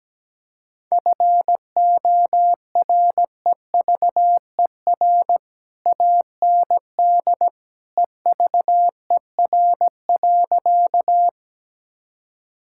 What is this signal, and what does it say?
Morse code, heard as a keyed tone, about 17 wpm, 700 Hz. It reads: FOREVER AND EVER.
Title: Sentences from Top 1600 Words 17wpm